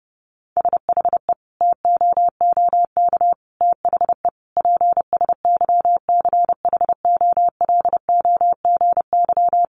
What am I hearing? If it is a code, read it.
Morse code, heard as a keyed tone, about 30 wpm, 700 Hz. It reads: SHE TOOK THE PSYCHOLOGY
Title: Sentences from Top 1400 Words 30wpm